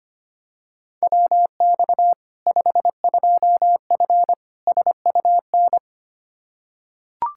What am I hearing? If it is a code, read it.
Morse code, heard as a keyed tone, about 25 wpm, 700 Hz. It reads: WX 52F SUN
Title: Practice QSO - Exchange 2 - Info - 25wpm